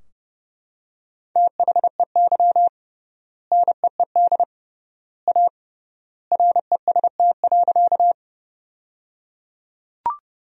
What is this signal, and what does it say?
Morse code, heard as a keyed tone, about 30 wpm, 700 Hz. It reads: THEY NEED A REST.
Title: 3x Word-Spacing - Sentences from Top 500 Words - 30wpm